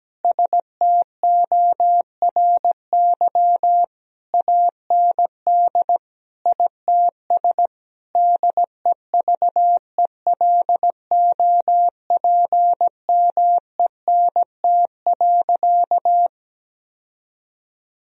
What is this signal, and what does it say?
Morse code, heard as a keyed tone, about 17 wpm, 700 Hz. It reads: STORY AND ITS DEVELOPMENT.
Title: Sentences from Top 1000 Words - Review Part 4 - 17wpm